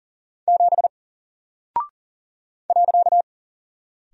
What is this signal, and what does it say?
Morse code, heard as a keyed tone, about 40 wpm, 700 Hz. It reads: 7 .
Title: Single Character (Letter:Number:Punctuation:Prosign) Alt 40wpm